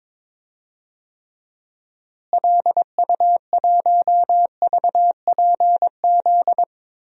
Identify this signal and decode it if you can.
Morse code, heard as a keyed tone, about 22 wpm, 700 Hz. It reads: LU1VPZ
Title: DXCC-100 Call Signs - 6 Characters - Repeated 2x - 22wpm